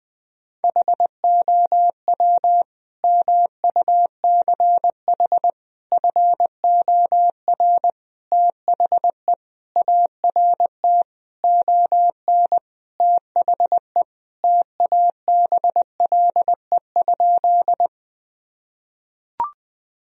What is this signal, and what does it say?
Morse code, heard as a keyed tone, about 20 wpm, 700 Hz. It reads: HOW MUCH FOR THE ART ON THE TABLE?
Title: Sentences from Top 1000 Words - Review Part 3 - 20wpm